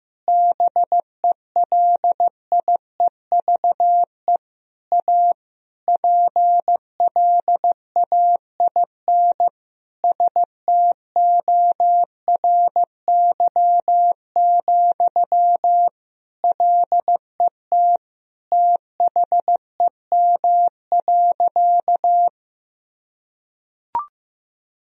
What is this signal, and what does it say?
Morse code, heard as a keyed tone, about 15 wpm, 700 Hz. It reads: BELIEVE A PLAIN STORY, LET THEM.
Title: Sentences from Top 1000 Words - Review Part 2 - 15wpm